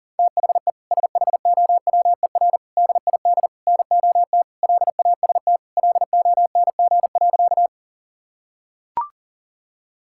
Morse code, heard as a keyed tone, about 40 wpm, 700 Hz. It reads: THE SHOWER DID NOT LAST LONG.
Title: Sentences from Top 1600 Words 40wpm